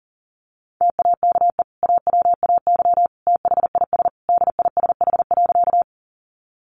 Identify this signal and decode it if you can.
Morse code, heard as a keyed tone, about 40 wpm, 700 Hz. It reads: TAKE AWAY THIS DISH.
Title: Sentences from Top 1300 Words 40wpm